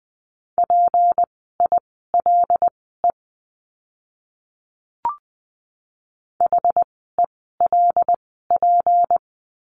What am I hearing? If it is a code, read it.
Morse code, heard as a keyed tone, about 20 wpm, spaced as for 15 wpm, 700 Hz. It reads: PILE HELP
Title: 153 Lesson 25f - Letter p - Words 20wpm